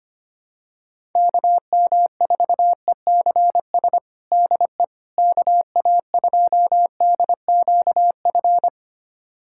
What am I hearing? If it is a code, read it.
Morse code, heard as a keyed tone, about 25 wpm, 700 Hz. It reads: KM4ECS DE KA2DQF
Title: USA Callsign DE Callsign - 6 Characters - 25wpm